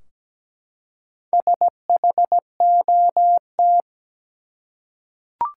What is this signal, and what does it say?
Morse code, heard as a keyed tone, about 17 wpm, 700 Hz. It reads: SHOT
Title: Four Letter Words - 17wpm